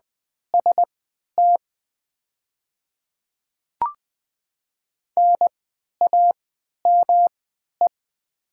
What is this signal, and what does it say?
Morse code, heard as a keyed tone, about 20 wpm, spaced as for 11 wpm, 700 Hz. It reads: ST NAME
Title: 3x Character Spacing - Top 200 Words - 20wpm